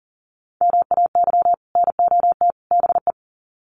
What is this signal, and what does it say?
Morse code, heard as a keyed tone, about 38 wpm, 700 Hz. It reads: MAY NOT BE
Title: Sets of 3 Words 40wpm